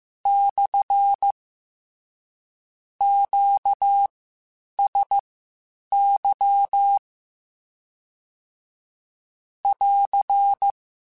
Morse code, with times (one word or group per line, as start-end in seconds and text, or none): 0.3-1.3 /
3.0-7.0 QSY
9.7-10.7 <AR>